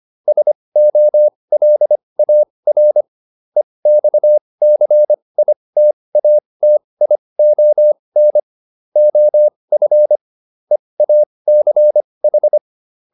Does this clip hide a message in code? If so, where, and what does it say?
Yes: Morse code from 0.3 to 12.6 s, heard as a keyed tone, about 25 wpm, spaced as for 20 wpm, 600 Hz. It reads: SOLAR EXCITATION OF EACH